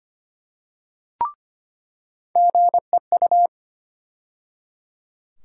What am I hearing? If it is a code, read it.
Morse code, heard as a keyed tone, about 25 wpm, 700 Hz. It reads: GEU
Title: USA Call Sign Postfixes - 3 Letters 25wpm